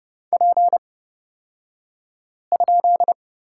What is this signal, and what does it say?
Morse code, heard as a keyed tone, about 30 wpm, spaced as for 5 wpm, 700 Hz. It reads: P?
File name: Lesson 37d - Letter x - Two Characters 30wpm